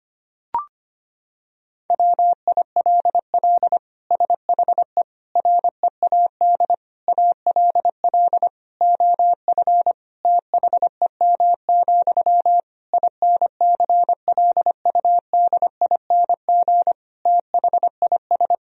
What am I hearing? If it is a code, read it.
Morse code, heard as a keyed tone, about 25 wpm, 700 Hz. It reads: WILL SHE READ ALL OF THEM, INCLUDING THIS
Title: Sentences from Top 1000 Words - Review Part 4 - 25wpm